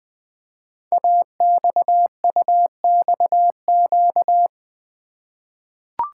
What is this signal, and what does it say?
Morse code, heard as a keyed tone, about 20 wpm, 700 Hz. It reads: AXUXQ
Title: Code Groups - 5 Letters - 20wpm